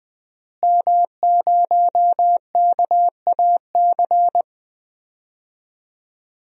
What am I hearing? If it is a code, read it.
Morse code, heard as a keyed tone, about 21 wpm, 700 Hz. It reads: M0KAC
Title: DXCC-100 Call Signs - 5 Characters - Repeated 3x - 20wpm